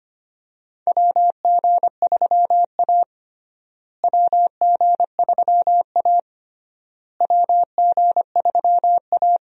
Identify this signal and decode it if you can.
Morse code, heard as a keyed tone, about 25 wpm, 700 Hz. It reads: WG3A WG3A WG3A
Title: USA Call Signs - 4 Characters - Repeated 3x - 25wpm